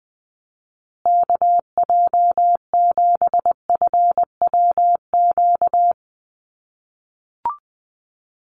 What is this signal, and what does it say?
Morse code, heard as a keyed tone, about 20 wpm, 700 Hz. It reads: KJ7FWQ
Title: USA Call Signs Repeated 2x 20wpm